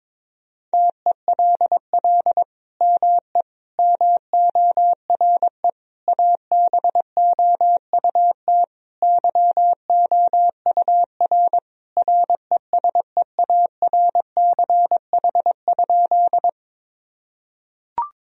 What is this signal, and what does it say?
Morse code, heard as a keyed tone, about 22 wpm, 700 Hz. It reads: TELL ME MORE ABOUT YOUR RESEARCH?
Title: Sentences from Top 1000 Words - Review Part 4 - 22wpm